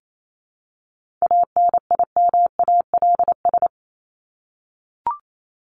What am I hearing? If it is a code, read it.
Morse code, heard as a keyed tone, about 28 wpm, 700 Hz. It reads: ANIMALS
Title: ES:S Suffix - Top 100 Words with Base Words 28wpm